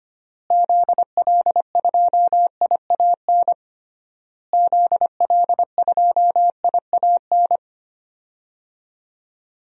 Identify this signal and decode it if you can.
Morse code, heard as a keyed tone, about 25 wpm, 700 Hz. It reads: ZL2IAN ZL2IAN
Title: DXCC-100 Call Signs - 6 Characters - Repeated 3x - 25wpm